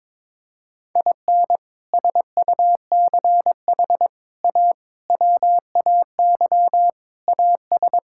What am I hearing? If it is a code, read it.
Morse code, heard as a keyed tone, about 21 wpm, 700 Hz. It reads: IN SUCH A WAY AS
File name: practice / 22wpm / Sets of 6 Words - 22wpm